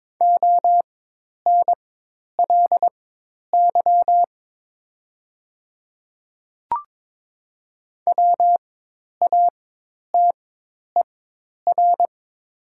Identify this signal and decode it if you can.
Morse code, heard as a keyed tone, about 22 wpm, spaced as for 10 wpm, 700 Hz. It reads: ONLY WATER
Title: 4x Character Spacing - Top 200 Words - 22wpm